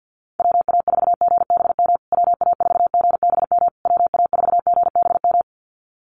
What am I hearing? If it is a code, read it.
Morse code, heard as a keyed tone, about 50 wpm, 700 Hz. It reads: WA4GBM WA4GBM WA4GBM
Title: USA Call Signs Repeated 3x 50wpm